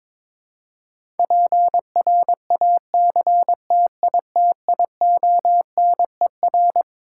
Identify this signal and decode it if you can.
Morse code, heard as a keyed tone, about 22 wpm, 700 Hz. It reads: PRACTITIONER
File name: Twelve Letter Words - 22wpm